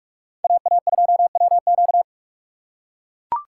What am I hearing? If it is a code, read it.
Morse code, heard as a keyed tone, about 42 wpm, 700 Hz. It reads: AA2WQ